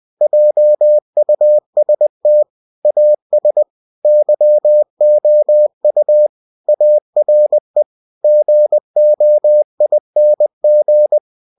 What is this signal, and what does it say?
Morse code, heard as a keyed tone, about 20 wpm, 600 Hz. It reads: JUST AS YOU ARE GOING
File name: cw-books / winnie-the-pooh / winnie-20wpm-page-0000